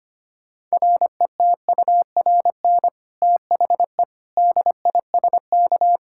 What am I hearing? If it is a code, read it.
Morse code, heard as a keyed tone, about 25 wpm, 700 Hz. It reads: RETURN THE DISK
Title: Sentences from Top 1000 Words - Review Part 6 - 25wpm